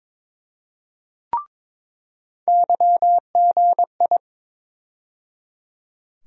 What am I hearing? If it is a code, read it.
Morse code, heard as a keyed tone, about 22 wpm, 700 Hz. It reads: YGI
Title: USA Call Sign Postfixes - 3 Letters 22wpm